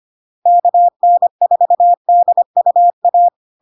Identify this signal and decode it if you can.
Morse code, heard as a keyed tone, about 25 wpm, 700 Hz. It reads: KN4DUA
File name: LARK-FD-2022-Log-in-CW-25wpm-700-Hz